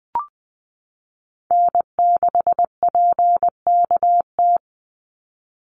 Morse code, heard as a keyed tone, about 20 wpm, 700 Hz. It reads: N6PKT
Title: NA SOTA Chaser Callsigns 20wpm